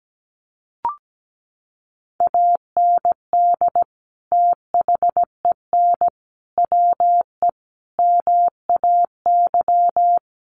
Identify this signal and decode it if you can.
Morse code, heard as a keyed tone, about 17 wpm, 700 Hz. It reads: AND THEN WE MAY
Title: Sentences from Top 300 Words 17wpm